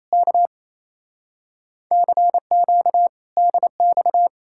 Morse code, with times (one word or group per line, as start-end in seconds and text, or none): 0.1-0.5 K
1.9-3.1 CQ
3.4-4.3 DX